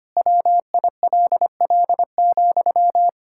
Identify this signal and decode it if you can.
Morse code, heard as a keyed tone, about 25 wpm, 700 Hz. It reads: WILL,
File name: practice / 25wpm / Difficult Sentences with Hints - 25wpm